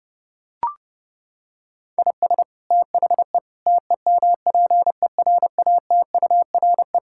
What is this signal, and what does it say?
Morse code, heard as a keyed tone, about 30 wpm, 700 Hz. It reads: IS THE TEMPERATURE
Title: Sentences from Top 1000 Words - Review Part 6 - 30wpm